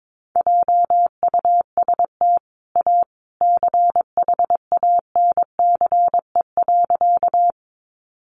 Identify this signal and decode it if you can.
Morse code, heard as a keyed tone, about 22 wpm, 700 Hz. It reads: JUST A CHANCE.
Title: Sentences from Top 800 Words 22wpm